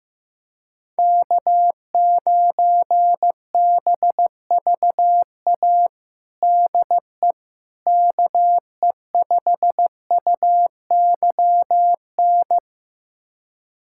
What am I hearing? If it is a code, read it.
Morse code, heard as a keyed tone, about 16 wpm, 700 Hz. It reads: K9BVA DE KE5UYN
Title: USA Callsign DE Callsign 15wpm